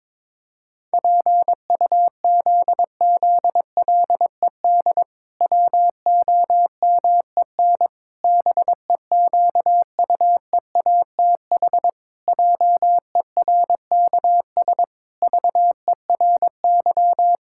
Morse code, heard as a keyed tone, about 22 wpm, 700 Hz. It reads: PUZZLED WOMEN BEQUEATH JERKS VERY
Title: Pangram Challenge v2 - 22wpm